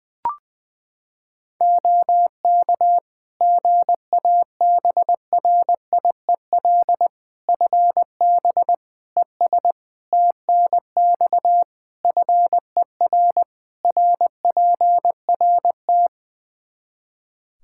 Morse code, heard as a keyed tone, about 20 wpm, 700 Hz. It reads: OK GABRIEL FB ES TNX FER RPRT
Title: Practice QSO - Exchange 2 - Info - 20wpm